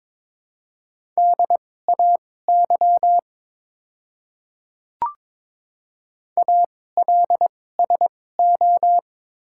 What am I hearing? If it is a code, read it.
Morse code, heard as a keyed tone, about 22 wpm, spaced as for 16 wpm, 700 Hz. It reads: DAY ALSO
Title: 2x Character Spacing - Top 200 Words - 22wpm